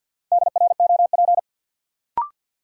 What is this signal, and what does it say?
Morse code, heard as a keyed tone, about 53 wpm, 700 Hz. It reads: DROP
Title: Sentences from Top 1400 Words - New Word Drill - 50wpm